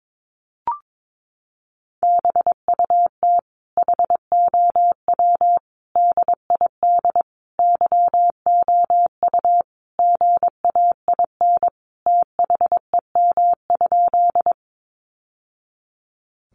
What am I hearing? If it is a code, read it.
Morse code, heard as a keyed tone, about 22 wpm, 700 Hz. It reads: BUT HOW DID YOU GAIN THEM?
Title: Sentences from Top 1000 Words 22wpm